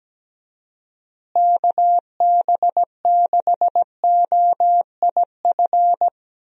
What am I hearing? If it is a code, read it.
Morse code, heard as a keyed tone, about 17 wpm, 700 Hz. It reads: KB6OIF
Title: USA Call Signs - 6 Characters - Repeated 2x - 17wpm